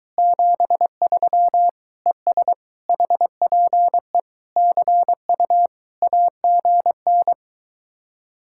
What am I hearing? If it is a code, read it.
Morse code, heard as a keyed tone, about 23 wpm, 700 Hz. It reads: 73 ES HPE CU AGN